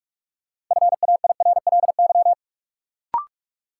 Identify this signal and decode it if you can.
Morse code, heard as a keyed tone, about 46 wpm, 700 Hz. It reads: FAIRLY
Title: Sentences from Top 1100 Words - New Word Drill - 45wpm